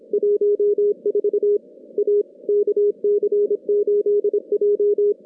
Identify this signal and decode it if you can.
Morse code, heard as a keyed tone, about 26 wpm, 410 Hz. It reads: 14 A KC8J
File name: contests / CW-SS-k5tr- -2100